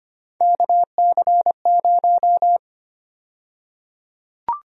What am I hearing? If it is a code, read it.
Morse code, heard as a keyed tone, about 25 wpm, 700 Hz. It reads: KC0
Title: USA Call Sign Prefixes - 25wpm